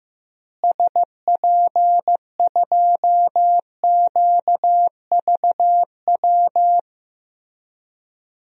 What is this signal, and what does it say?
Morse code, heard as a keyed tone, about 16 wpm, 700 Hz. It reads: SP2QVW
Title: DXCC-100 Call Signs - 6 Characters - Repeated 3x - 15wpm